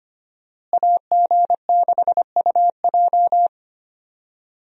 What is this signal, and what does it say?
Morse code, heard as a keyed tone, about 25 wpm, 700 Hz. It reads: AG6UJ
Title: USA Call Signs - 5 Characters - Repeated 2x - 25wpm